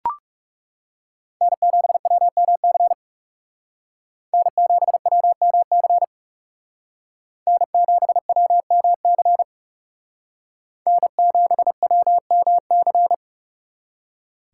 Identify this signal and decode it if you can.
Morse code, heard as a keyed tone, about 36 wpm, 700 Hz. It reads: N7WMC N7WMC N7WMC N7WMC
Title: Speed Racing - USA 1x3 Call Signs 30wpm